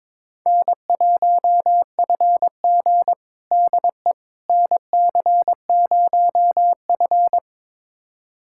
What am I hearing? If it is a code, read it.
Morse code, heard as a keyed tone, about 22 wpm, 700 Hz. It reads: N1FG DE NC0F
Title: USA Callsign DE Callsign - 4 Characters - 22wpm